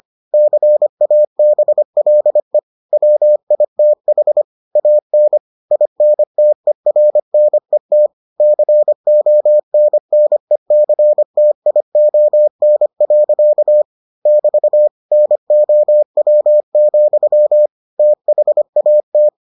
Morse code, heard as a keyed tone, about 25 wpm, 600 Hz. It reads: CABLE WITH AN INTERNET CONNECTION. <BT> NOW, THAT